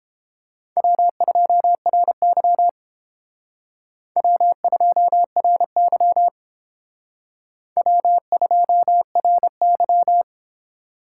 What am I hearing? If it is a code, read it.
Morse code, heard as a keyed tone, about 30 wpm, 700 Hz. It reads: W2RY W2RY W2RY